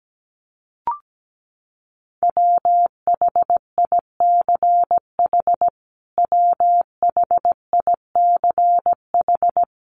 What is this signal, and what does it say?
Morse code, heard as a keyed tone, about 17 wpm, 700 Hz. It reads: WHICH WHICH
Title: Top 500 Words Repeated 17wpm